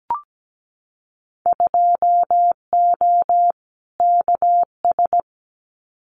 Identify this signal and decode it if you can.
Morse code, heard as a keyed tone, about 17 wpm, 700 Hz. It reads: 2O KS
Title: Winter Field Day Exchanges Encore 17wpm